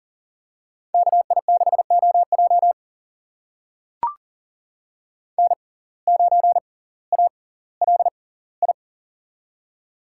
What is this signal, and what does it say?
Morse code, heard as a keyed tone, about 40 wpm, spaced as for 20 wpm, 700 Hz. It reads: KI6OJ N9ALI
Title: USA Call Signs - 5 Characters - 6x Character Spacing - 40wpm